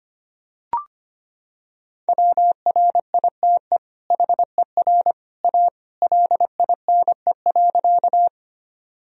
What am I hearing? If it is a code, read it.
Morse code, heard as a keyed tone, about 25 wpm, 700 Hz. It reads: WRITE HER A LINE.
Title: Sentences from Top 1000 Words - Review Part 1 - 25wpm